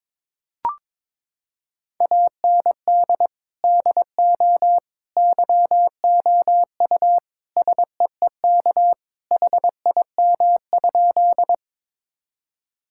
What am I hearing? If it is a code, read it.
Morse code, heard as a keyed tone, about 22 wpm, 700 Hz. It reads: AND DO YOU SEEK HIM?
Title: Sentences from Top 1600 Words 22wpm